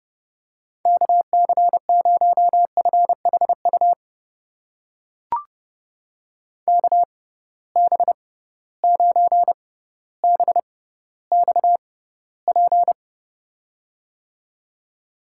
Morse code, heard as a keyed tone, about 32 wpm, spaced as for 16 wpm, 700 Hz. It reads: KC0FHU KB9BXP
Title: USA Call Signs - 6 Characters - 6x Character Spacing - 30wpm